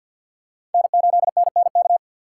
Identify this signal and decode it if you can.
Morse code, heard as a keyed tone, about 51 wpm, 700 Hz. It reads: N8NNK
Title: USA Call Signs - 5 Characters - 4x Character Spacing - 50wpm